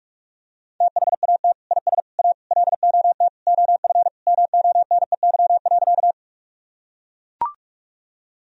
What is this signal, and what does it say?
Morse code, heard as a keyed tone, about 41 wpm, 700 Hz. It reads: THAT IS A LOT OF MONEY.
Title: Sentences from Top 1000 Words - Review Part 3 - 45wpm